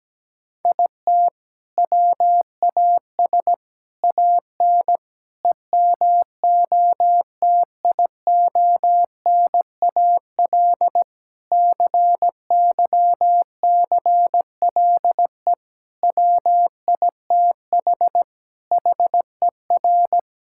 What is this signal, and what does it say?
Morse code, heard as a keyed tone, about 17 wpm, 700 Hz. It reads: IT WAS AN EMOTIONAL CYCLE WITH HER
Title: Sentences from Top 1000 Words - Review Part 6 - 17wpm